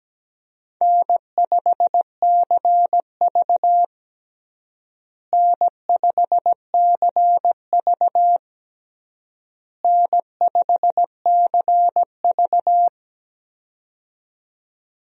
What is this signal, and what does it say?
Morse code, heard as a keyed tone, about 17 wpm, 700 Hz. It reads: N5CV N5CV N5CV
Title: USA Call Signs - 4 Characters - Repeated 3x - 17wpm